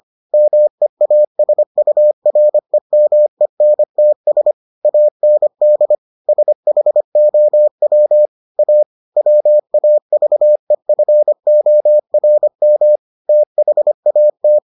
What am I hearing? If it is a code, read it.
Morse code, heard as a keyed tone, about 25 wpm, 600 Hz. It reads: MEASUREMENTS AND SHOW A WAVEFORM THAT